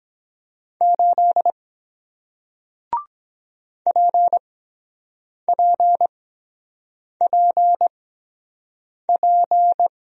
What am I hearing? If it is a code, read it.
Morse code, heard as a keyed tone, about 23 wpm, spaced as for 7 wpm, 700 Hz. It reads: 8 PPPP